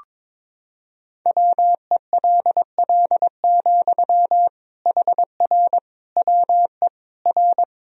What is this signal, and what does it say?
Morse code, heard as a keyed tone, about 22 wpm, 700 Hz. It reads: WELL, HR WE R
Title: Sentences from Top 300 Words and Abbreviations 22wpm